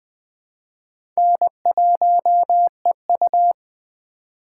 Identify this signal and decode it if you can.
Morse code, heard as a keyed tone, about 21 wpm, 700 Hz. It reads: N1EU
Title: 2023 SOTA Chaser Call Signs - 20wpm